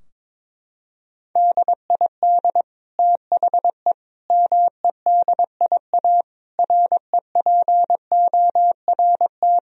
Morse code, heard as a keyed tone, about 22 wpm, 700 Hz. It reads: DID THE MEDIA REPORT
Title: Sentences from Top 1000 Words - Review Part 4 - 22wpm